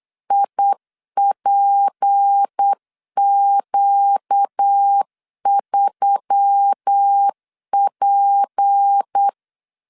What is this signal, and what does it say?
Morse code, heard as a keyed tone, about 8 wpm, 800 Hz. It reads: IPQ3P
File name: LEZIONI / 6 PKX0 / 6 P 8wpm 3025